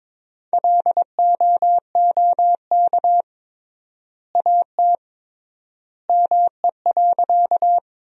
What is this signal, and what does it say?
Morse code, heard as a keyed tone, about 21 wpm, 700 Hz. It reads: LOOK AT ME.
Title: 3x Word-Spacing - Sentences from Top 200 Words - 22wpm